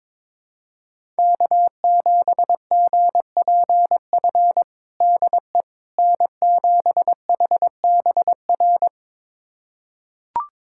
Morse code, heard as a keyed tone, about 22 wpm, 700 Hz. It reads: K7GPF DE N7HBR
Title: USA Callsign DE Callsign - 5 Characters - 22wpm